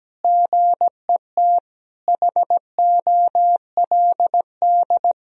Morse code, heard as a keyed tone, about 17 wpm, 700 Hz. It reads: GET HOLD
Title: Sentences from Top 1000 Words - Review Part 2 - 17wpm